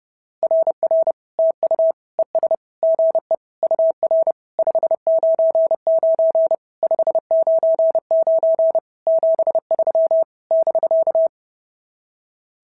Morse code, heard as a keyed tone, about 30 wpm, 650 Hz. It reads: RR TU ES GE UR 599 599 73 <BK>
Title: SOTA QSO Practice 30wpm